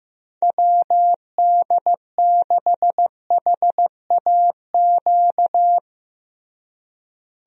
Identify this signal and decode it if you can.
Morse code, heard as a keyed tone, about 16 wpm, 700 Hz. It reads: WD6HAQ
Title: USA Call Signs - 6 Characters - Repeated 2x - 15wpm